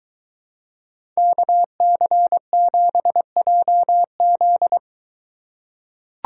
Morse code, heard as a keyed tone, about 23 wpm, 700 Hz. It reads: KC7JZ